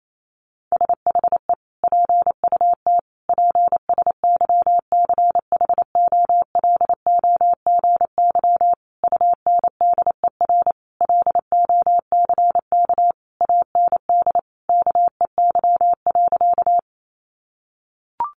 Morse code, heard as a keyed tone, about 28 wpm, 700 Hz. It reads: SHE PUT PSYCHOLOGY UNDER LOCK AND KEY.
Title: Sentences from Top 1400 Words 28wpm